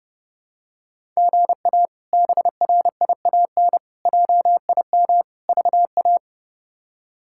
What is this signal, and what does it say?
Morse code, heard as a keyed tone, about 30 wpm, 700 Hz. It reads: GA BRIAN JIM VA
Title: K1USN SST QSO Practice - 30wpm